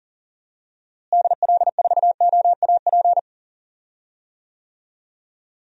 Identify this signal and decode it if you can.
Morse code, heard as a keyed tone, about 40 wpm, 700 Hz. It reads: DL4OAP
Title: USA Call Signs - 6 Characters - Repeated 3x - 40wpm